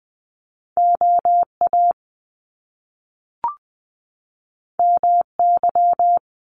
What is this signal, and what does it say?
Morse code, heard as a keyed tone, about 19 wpm, 700 Hz. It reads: OA MY
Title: Top Two Letter Combinations in Words 20wpm